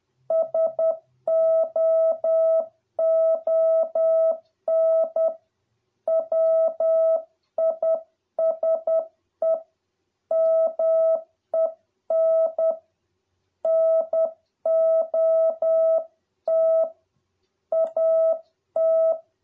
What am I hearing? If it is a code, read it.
Morse code, heard as a keyed tone, about 10 wpm, 650 Hz. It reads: SOON WISE MEN NOT AT